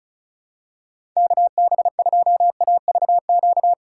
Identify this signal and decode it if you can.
Morse code, heard as a keyed tone, about 36 wpm, 700 Hz. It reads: KB2AVQ
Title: USA Call Signs - 6 Characters - 3x Character Spacing - 35wpm